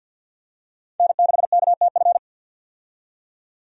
Morse code, heard as a keyed tone, about 49 wpm, 700 Hz. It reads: N6KTF
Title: USA Call Signs - 5 Characters - 50wpm